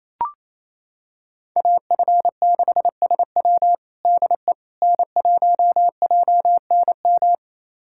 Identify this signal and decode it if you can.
Morse code, heard as a keyed tone, about 27 wpm, 700 Hz. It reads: AF6SW DE N1JNM
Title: USA Callsign DE Callsign - 5 Characters - 28wpm